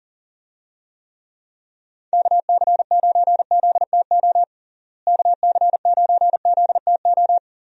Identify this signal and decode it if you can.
Morse code, heard as a keyed tone, about 40 wpm, 700 Hz. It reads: KC9ZTO KC9ZTO
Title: USA Call Signs - 6 Characters - Repeated 3x - 40wpm